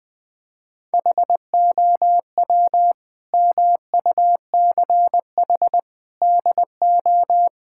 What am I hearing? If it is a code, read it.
Morse code, heard as a keyed tone, about 20 wpm, 700 Hz. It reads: HOW MUCH DO
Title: Sentences from Top 1000 Words - Review Part 6 - 20wpm